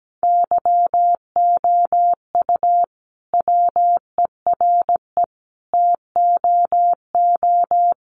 Morse code, heard as a keyed tone, about 17 wpm, 700 Hz. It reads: YOU WERE TOO
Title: Sentences from Top 1500 Words 17wpm